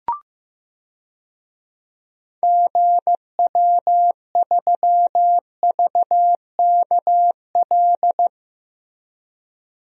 Morse code, heard as a keyed tone, about 15 wpm, 700 Hz. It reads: GW3VKL
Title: DXCC-100 Call Signs - 6 Characters - Repeated 3x - 15wpm